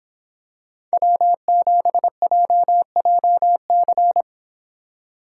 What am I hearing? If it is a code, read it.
Morse code, heard as a keyed tone, about 24 wpm, 700 Hz. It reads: W7JJC